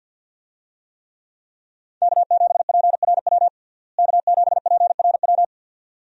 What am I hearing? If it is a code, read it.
Morse code, heard as a keyed tone, about 51 wpm, 700 Hz. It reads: K7PRW K7PRW
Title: USA Call Signs - 5 Characters - Repeated 2x - 50wpm